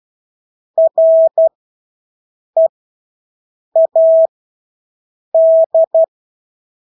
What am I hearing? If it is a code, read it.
Morse code, heard as a keyed tone, about 12 wpm, spaced as for 6 wpm, 650 Hz. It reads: READ
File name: Engelske ord 12-6 wpm